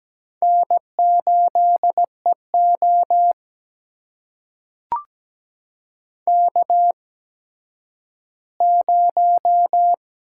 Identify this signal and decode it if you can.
Morse code, heard as a keyed tone, about 17 wpm, spaced as for 10 wpm, 700 Hz. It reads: N8EO K0
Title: USA Call Signs - 4 Characters - 8x Character Spacing - 17wpm